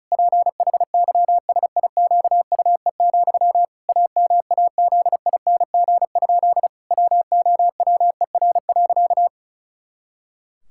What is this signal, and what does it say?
Morse code, heard as a keyed tone, about 36 wpm, 700 Hz. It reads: PHYSIQUE, AMAZING? WOWER.
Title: Pangram Challenge v2 - 35wpm